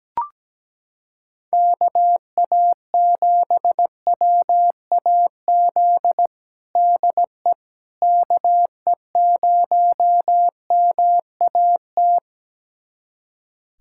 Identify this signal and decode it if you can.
Morse code, heard as a keyed tone, about 17 wpm, 700 Hz. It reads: KA7WAZ DE KE0MAT
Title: USA Callsign DE Callsign - 6 Characters - 17wpm